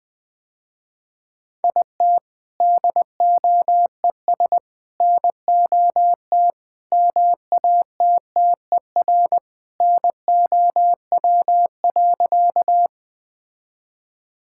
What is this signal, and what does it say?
Morse code, heard as a keyed tone, about 21 wpm, 700 Hz. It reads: IT DOES NOT MATTER NOW.
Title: Sentences with 4-Word Phrases - Spoken Beginning - 20wpm